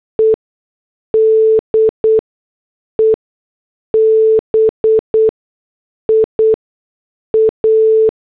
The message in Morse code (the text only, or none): E DEBIA